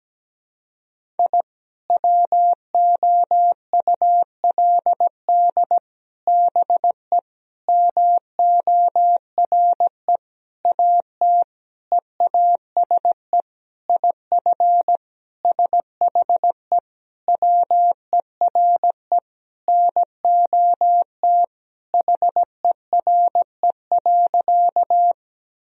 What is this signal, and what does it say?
Morse code, heard as a keyed tone, about 17 wpm, 700 Hz. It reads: I WOULD BE MORE AT EASE IF SHE WERE NOT HERE.
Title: Sentences from Top 1000 Words - Review Part 1 - 17wpm